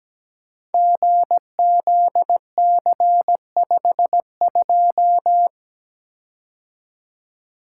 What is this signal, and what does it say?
Morse code, heard as a keyed tone, about 17 wpm, 700 Hz. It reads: GZC52
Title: Code Groups - 5 Letter:Numbers - 17wpm